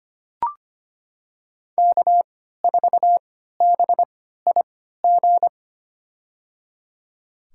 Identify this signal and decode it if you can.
Morse code, heard as a keyed tone, about 25 wpm, spaced as for 14 wpm, 700 Hz. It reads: K4BIG
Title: USA Call Signs - 5 Characters - 3x Character Spacing - 25wpm